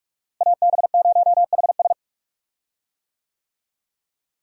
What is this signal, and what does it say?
Morse code, heard as a keyed tone, about 46 wpm, 700 Hz. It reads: AB0HS
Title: USA Call Signs - 5 Characters - Repeated 2x - 45wpm